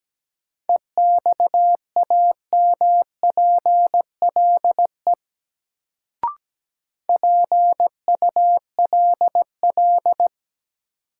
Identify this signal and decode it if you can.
Morse code, heard as a keyed tone, about 17 wpm, 700 Hz. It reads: EXAMPLE PULL
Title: Sentences from Top 400 Words - New Word Drill - 17wpm